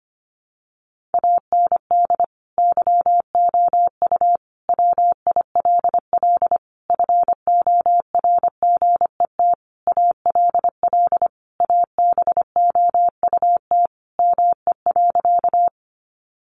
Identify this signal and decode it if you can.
Morse code, heard as a keyed tone, about 25 wpm, 700 Hz. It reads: AND YOU WILL FORGET ALL ABOUT ME.
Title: Sentences from Top 1200 Words 25wpm